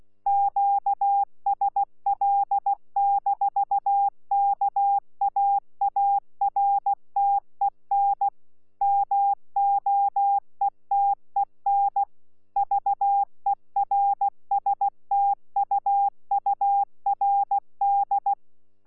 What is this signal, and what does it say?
Morse code, heard as a keyed tone, about 16 wpm, 800 Hz. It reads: QSL-KAARTEN MOETEN VERSTUURD
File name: Z PI4VRZ / PI4VRZ 14-jun-2025